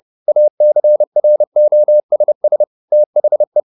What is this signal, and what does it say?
Morse code, heard as a keyed tone, about 30 wpm, 600 Hz. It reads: ACROSS THE